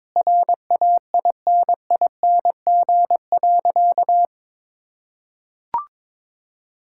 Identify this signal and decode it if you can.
Morse code, heard as a keyed tone, about 21 wpm, 700 Hz. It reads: RAINING.
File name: practice / 22wpm / Sentences with 3-Word Phrases - 22wpm